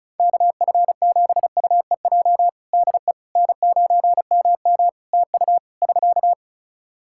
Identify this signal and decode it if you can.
Morse code, heard as a keyed tone, about 34 wpm, 700 Hz. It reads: KF7UEJ DE N9MM TU <SK>
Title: Practice QSO - Exchange 4 - Ending - 35wpm